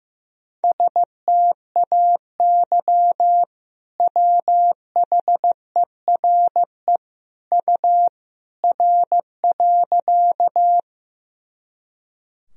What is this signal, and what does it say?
Morse code, heard as a keyed tone, about 16 wpm, 700 Hz. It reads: STAY WHERE U R.
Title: Sentences from Top 500 Words and Abbreviations 15wpm